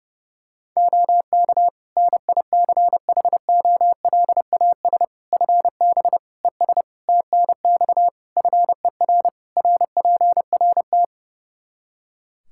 Morse code, heard as a keyed tone, about 30 wpm, 700 Hz. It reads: OK NICHOLAS FB ES TNX FER RPRT
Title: Practice QSO - Exchange 2 - Info - 30wpm